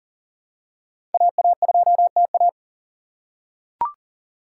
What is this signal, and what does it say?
Morse code, heard as a keyed tone, about 38 wpm, 700 Hz. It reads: AA2TA
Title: USA Call Signs - 3x Character Spacing - 40wpm